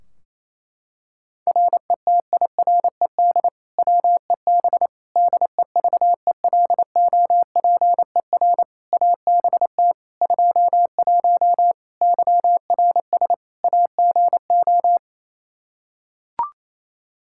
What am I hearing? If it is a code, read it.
Morse code, heard as a keyed tone, about 28 wpm, 700 Hz. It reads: RETIRED WEB DEVELOPER ABT 21 YRS AGO
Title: Practice QSO - Exchange 3 - More Info - 28wpm